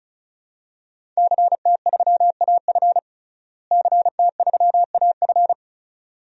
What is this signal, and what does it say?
Morse code, heard as a keyed tone, about 36 wpm, 700 Hz. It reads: CT3AF CT3AF
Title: DXCC-100 Call Signs - 5 Characters - Repeated 2x - 35wpm